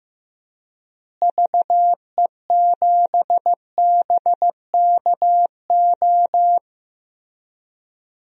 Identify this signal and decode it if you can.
Morse code, heard as a keyed tone, about 15 wpm, 700 Hz. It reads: VE7BKO
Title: DXCC-100 Call Signs - 6 Characters - Repeated 3x - 15wpm